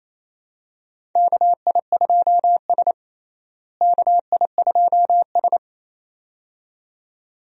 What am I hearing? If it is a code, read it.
Morse code, heard as a keyed tone, about 28 wpm, 700 Hz. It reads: KI2S KI2S
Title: USA Call Signs - 4 Characters - Repeated 2x - 28wpm